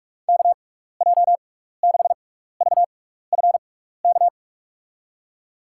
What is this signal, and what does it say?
Morse code, heard as a keyed tone, about 43 wpm, spaced as for 15 wpm, 700 Hz. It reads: KJ6VFK
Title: USA Call Signs - 6 Characters - 6x Character Spacing - 45wpm